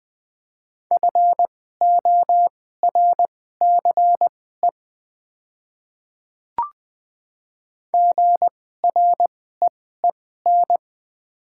Morse code, heard as a keyed tone, about 20 wpm, spaced as for 14 wpm, 700 Hz. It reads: FORCE GREEN
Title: 2x Character Spacing - Top 500 Words - 20wpm